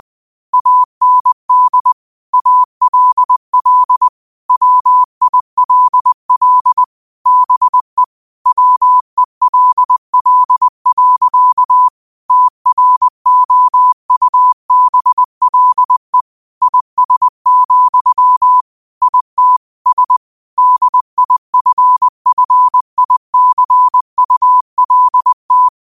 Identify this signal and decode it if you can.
Morse code, heard as a keyed tone, about 20 wpm, 1000 Hz. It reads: AND ALL WILL BE WELL. TROUBLE IS, IT S DIFFICULT
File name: 20wpm